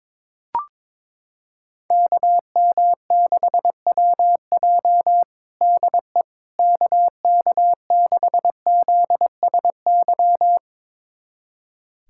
Morse code, heard as a keyed tone, about 21 wpm, 700 Hz. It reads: KM6WJ DE KK6ZSY
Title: USA Callsign DE Callsign 22wpm